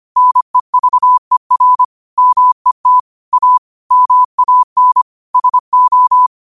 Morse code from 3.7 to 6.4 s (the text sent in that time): MAN SO